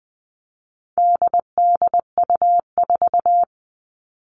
Morse code, heard as a keyed tone, about 20 wpm, 700 Hz. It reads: DDU4
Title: All Cut Numbers - 4-Digit 20wpm